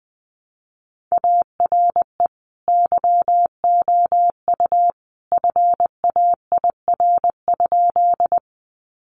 Morse code, heard as a keyed tone, about 20 wpm, 700 Hz. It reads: ARE YOU FAIR?
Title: Sentences from Top 1300 Words 20wpm